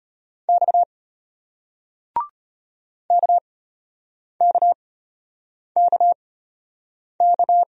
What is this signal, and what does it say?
Morse code, heard as a keyed tone, about 33 wpm, spaced as for 8 wpm, 700 Hz. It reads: X KKKK